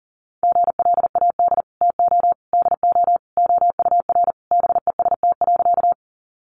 Morse code, heard as a keyed tone, about 40 wpm, 700 Hz. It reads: GLAD TO DO OUR BEST.
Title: Sentences from Top 800 Words 40wpm